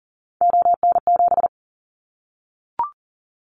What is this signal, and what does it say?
Morse code, heard as a keyed tone, about 40 wpm, 700 Hz. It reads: ON7
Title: Callsign Prefixes for Top 25 DXCC - Repeated 3x - 40wpm